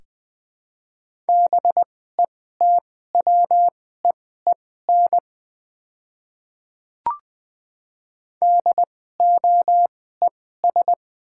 Morse code, heard as a keyed tone, about 20 wpm, spaced as for 15 wpm, 700 Hz. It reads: BETWEEN DOES
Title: 2x Character Spacing - Top 300 Words - 20wpm